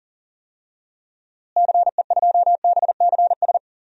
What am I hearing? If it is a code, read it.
Morse code, heard as a keyed tone, about 40 wpm, 700 Hz. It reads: CE2BCS
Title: DXCC-100 Call Signs - 6 Characters - Repeated 2x - 40wpm